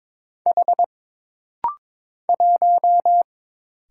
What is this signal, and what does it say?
Morse code, heard as a keyed tone, about 22 wpm, 700 Hz. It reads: H 1
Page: This rapid-fire timing is geared to push your ICR (Instant Character Recognition) skills.